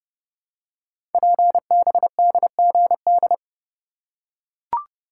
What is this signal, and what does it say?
Morse code, heard as a keyed tone, about 30 wpm, 700 Hz. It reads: PBDGD
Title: Code Groups - 5 Letters - 30wpm